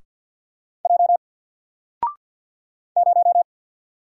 Morse code, heard as a keyed tone, about 48 wpm, 700 Hz. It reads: J 0
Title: Single Character (Letter:Number:Punctuation:Prosign) 50wpm